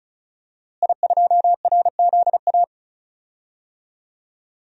Morse code, heard as a keyed tone, about 36 wpm, 700 Hz. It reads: I2RZA
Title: DXCC-100 Call Signs - 5 Characters - Repeated 2x - 35wpm